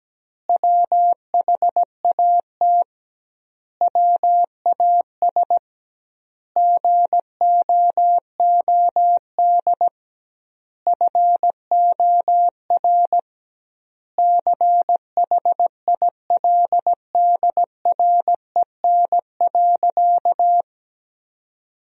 Morse code, heard as a keyed tone, about 17 wpm, 700 Hz. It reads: WHAT WAS GOOD FOR CHILDREN.
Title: 2x Word-Spacing - Sentences from Top 400 Words - 17wpm